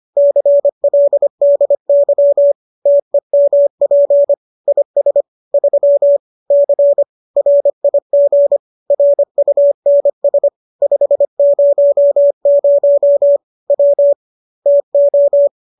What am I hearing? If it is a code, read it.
Morse code, heard as a keyed tone, about 25 wpm, 570 Hz. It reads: CLDY TEMP IS 3 C RIG RUNS 500 W TO